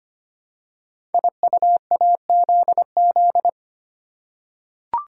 Morse code, heard as a keyed tone, about 25 wpm, 690 Hz. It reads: IUAZZ
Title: Code Groups - 5 Letters - 25wpm